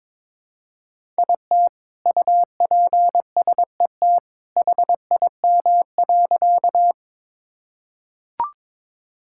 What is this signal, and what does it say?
Morse code, heard as a keyed tone, about 22 wpm, 700 Hz. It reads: IT UPSET HIM.
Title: Sentences from Top 1000 Words - Review Part 5 - 22wpm